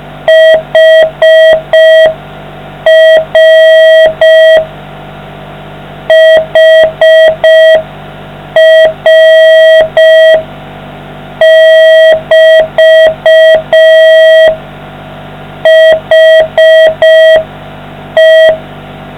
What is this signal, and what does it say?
Morse code, heard as a keyed tone, about 5 wpm, 640 Hz. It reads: HR HR<BT> HE